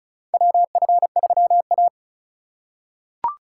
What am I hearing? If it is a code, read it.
Morse code, heard as a keyed tone, about 36 wpm, 700 Hz. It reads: WF3A
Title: USA Call Signs - 4 Characters - 8x Character Spacing - 35wpm